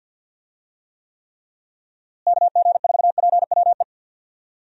Morse code, heard as a keyed tone, about 50 wpm, 700 Hz. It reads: KG4PWE
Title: USA Call Signs - 6 Characters - Repeated 3x - 50wpm